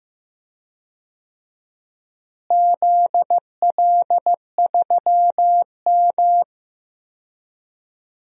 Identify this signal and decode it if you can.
Morse code, heard as a keyed tone, about 15 wpm, 700 Hz. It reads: ZL3M
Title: DXCC-100 Call Signs - 4 Characters - Repeated 2x - 15wpm